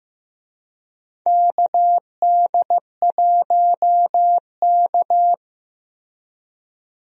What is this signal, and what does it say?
Morse code, heard as a keyed tone, about 15 wpm, 700 Hz. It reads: KD1K
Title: USA Call Signs - 4 Characters - Repeated 3x - 15wpm